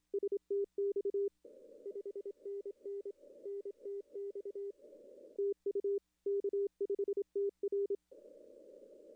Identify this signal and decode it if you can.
Morse code, heard as a keyed tone, about 26 wpm, 410 Hz. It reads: STX 5NN NTX TU K5TR
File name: contests / arrl-160-k5tr-op-w5tn -0400